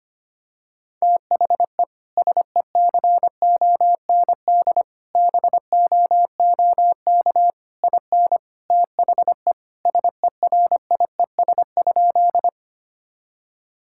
Morse code, heard as a keyed tone, about 25 wpm, 700 Hz. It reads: THE SECOND BOOK IN THE SERIES?
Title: Sentences from Top 1000 Words - Review Part 5 - 25wpm